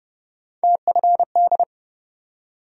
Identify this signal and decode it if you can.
Morse code, heard as a keyed tone, about 30 wpm, 700 Hz. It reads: TFD
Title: USA Call Sign Postfixes - 3 Letters 30wpm